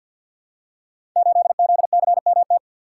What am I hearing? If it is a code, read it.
Morse code, heard as a keyed tone, about 51 wpm, 700 Hz. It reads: 8ZCMT
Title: Code Groups - 5 Letter:Numbers - 50wpm